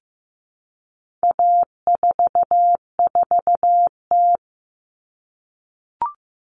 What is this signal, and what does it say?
Morse code, heard as a keyed tone, about 15 wpm, 700 Hz. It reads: A44T
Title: All Cut Numbers - 4-Digit 15wpm